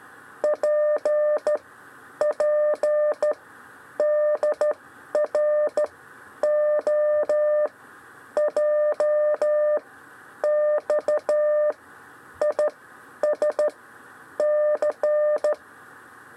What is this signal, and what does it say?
Morse code, heard as a keyed tone, about 14 wpm, spaced as for 9 wpm, 600 Hz. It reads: PPDRO JXISC
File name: 6 lezione cw 1